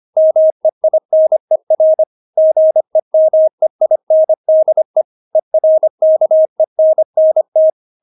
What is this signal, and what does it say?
Morse code, heard as a keyed tone, about 25 wpm, 630 Hz. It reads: MEINER GEMEINDE ERKENNT
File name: Witz6 25wpm